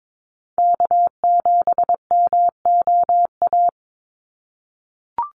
Morse code, heard as a keyed tone, about 22 wpm, 700 Hz. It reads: K7MOA
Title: Call Signs - 22wpm